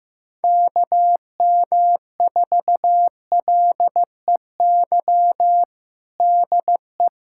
Morse code, heard as a keyed tone, about 15 wpm, 700 Hz. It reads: KM4LEY DE
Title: USA Callsign DE Callsign 15wpm